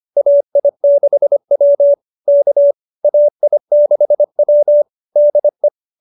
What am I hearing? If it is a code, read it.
Morse code, heard as a keyed tone, about 25 wpm, 590 Hz. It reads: AI6W K AI6W DE